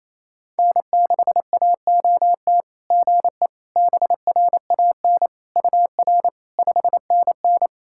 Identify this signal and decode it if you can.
Morse code, heard as a keyed tone, about 28 wpm, 690 Hz. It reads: N6AOT GE BRAN UR 5NN
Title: SOTA QSO Practice 28wpm